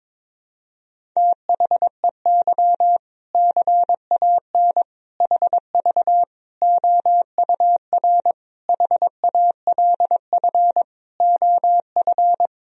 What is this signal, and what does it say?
Morse code, heard as a keyed tone, about 22 wpm, 700 Hz. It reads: THEY CAN HV OUR HALF OF
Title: Sentences from Top 400 Words and Abbreviations 22wpm